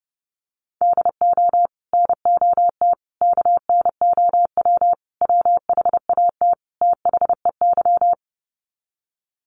Morse code, heard as a keyed tone, about 30 wpm, 700 Hz. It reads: DO NOT KNOW WHAT THEY
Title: Sets of 5 Words 30wpm